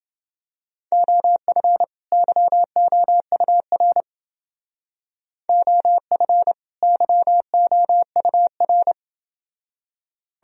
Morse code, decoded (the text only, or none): OF YOUR OF YOUR